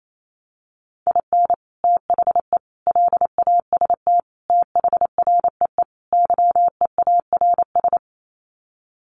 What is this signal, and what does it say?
Morse code, heard as a keyed tone, about 28 wpm, 700 Hz. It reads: IN THE LAST THREE YEARS
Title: Sets of 5 Words Encore2 28wpm